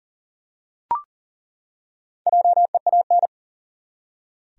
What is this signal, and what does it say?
Morse code, heard as a keyed tone, about 40 wpm, 700 Hz. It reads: JEAN
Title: USA Common Names - Max 4 Letters - 40wpm